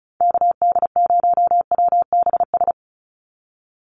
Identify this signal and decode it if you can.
Morse code, heard as a keyed tone, about 34 wpm, 700 Hz. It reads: KD0WBS
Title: USA Call Signs 35wpm